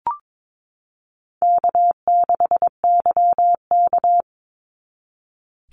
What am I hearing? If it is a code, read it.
Morse code, heard as a keyed tone, about 22 wpm, 700 Hz. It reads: K6YK
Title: NA SOTA Chaser Callsigns 22wpm